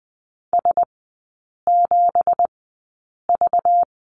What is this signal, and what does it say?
Morse code, heard as a keyed tone, about 20 wpm, spaced as for 8 wpm, 700 Hz. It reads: S7V
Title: 191 Lesson 31e - Letter v - Three Characters 20wpm